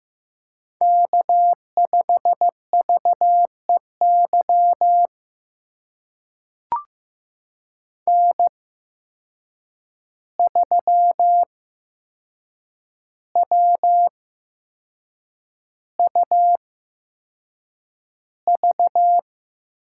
Morse code, heard as a keyed tone, about 15 wpm, spaced as for 6 wpm, 700 Hz. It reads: K5VEY N3WUV
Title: USA Call Signs - 5 Characters - 8x Character Spacing - 15wpm